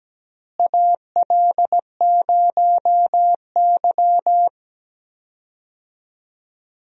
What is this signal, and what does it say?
Morse code, heard as a keyed tone, about 17 wpm, 700 Hz. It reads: AL0Y
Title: USA Call Signs - 4 Characters - Repeated 3x - 17wpm